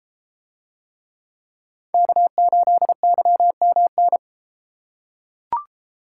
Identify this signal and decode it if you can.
Morse code, heard as a keyed tone, about 31 wpm, 700 Hz. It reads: K8YMN